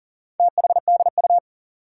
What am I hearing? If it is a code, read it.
Morse code, heard as a keyed tone, about 39 wpm, 700 Hz. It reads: THDU
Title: Four Challenging Letter-Number Combinations 40wpm